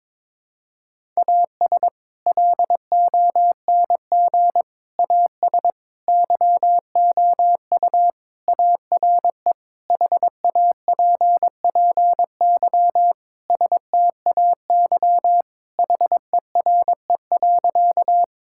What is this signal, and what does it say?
Morse code, heard as a keyed tone, about 22 wpm, 700 Hz. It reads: AS LONG AS YOU ARE HAPPY STAY HERE.
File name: practice / 22wpm / Sentences with 4-Word Phrases - 22wpm